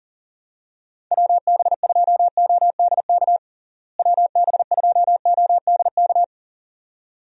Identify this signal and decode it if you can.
Morse code, heard as a keyed tone, about 40 wpm, 700 Hz. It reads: WB2ODK WB2ODK
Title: USA Call Signs - 6 Characters - Repeated 2x - 40wpm